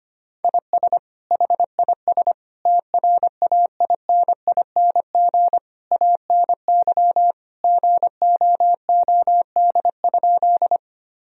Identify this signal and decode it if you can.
Morse code, heard as a keyed tone, about 25 wpm, 700 Hz. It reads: IS HIS TRAINING ANY GOOD?
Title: Sentences from Top 1000 Words - Review Part 4 - 25wpm